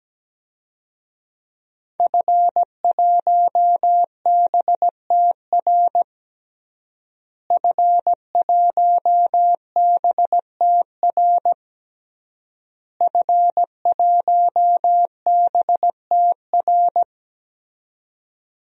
Morse code, heard as a keyed tone, about 17 wpm, 700 Hz. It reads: F1BTR F1BTR F1BTR
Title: DXCC-100 Call Signs - 5 Characters - Repeated 3x - 17wpm